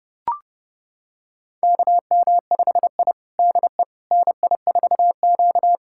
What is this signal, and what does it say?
Morse code, heard as a keyed tone, about 30 wpm, 700 Hz. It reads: KM5I DE NI4Q
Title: USA Callsign DE Callsign - 4 Characters - 30wpm